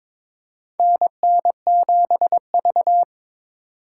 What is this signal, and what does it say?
Morse code, heard as a keyed tone, about 22 wpm, 700 Hz. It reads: NN7V
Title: USA Call Signs - 4x Character Spacing - 22wpm